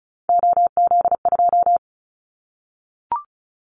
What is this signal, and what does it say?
Morse code, heard as a keyed tone, about 36 wpm, 690 Hz. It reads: OZ2
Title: Callsign Prefixes for Top 50 DXCC - 35wpm